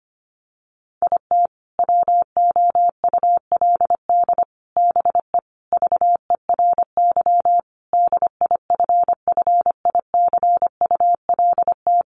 Morse code, heard as a keyed tone, about 25 wpm, 700 Hz. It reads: IT WOULD BE VERY DIFFICULT
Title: Sentences from Top 1100 Words 25wpm